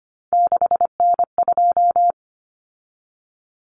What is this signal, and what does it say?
Morse code, heard as a keyed tone, about 25 wpm, 700 Hz. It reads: 6N2
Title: Common Cut Numbers - 3-Digit 25wpm